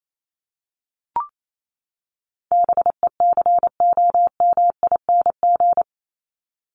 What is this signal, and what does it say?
Morse code, heard as a keyed tone, about 28 wpm, 700 Hz. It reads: BECOMING
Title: ING Suffix - Top 100 Words with Base Words 28wpm